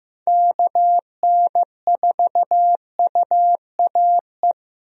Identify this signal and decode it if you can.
Morse code, heard as a keyed tone, about 15 wpm, 700 Hz. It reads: KN4UAE
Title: USA Call Signs - 6 Characters - Repeated 3x - 15wpm